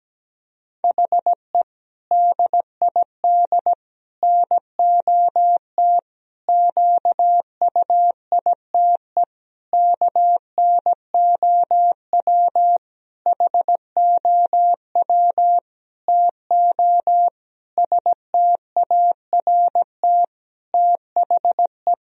Morse code, heard as a keyed tone, about 17 wpm, 700 Hz. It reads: HE DID NOT QUITE KNOW HOW TO START THE
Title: Sentences from Top 1000 Words - Review Part 1 - 17wpm